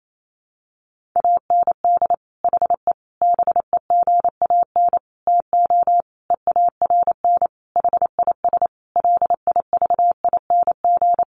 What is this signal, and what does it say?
Morse code, heard as a keyed tone, about 28 wpm, 700 Hz. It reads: AND HE BEGAN TO EARN HIS LIVING
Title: Sentences from Top 1500 Words 28wpm